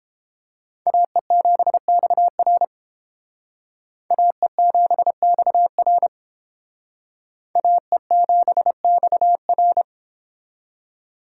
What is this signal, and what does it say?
Morse code, heard as a keyed tone, about 31 wpm, 700 Hz. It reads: AE7XR AE7XR AE7XR